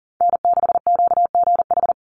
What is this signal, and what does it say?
Morse code, heard as a keyed tone, about 40 wpm, 700 Hz. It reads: N6QGH
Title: USA Call Signs 40wpm